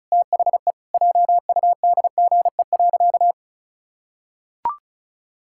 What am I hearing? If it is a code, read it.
Morse code, heard as a keyed tone, about 34 wpm, 700 Hz. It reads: THE JUDGE.
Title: Pangram Challenge v2 - 35wpm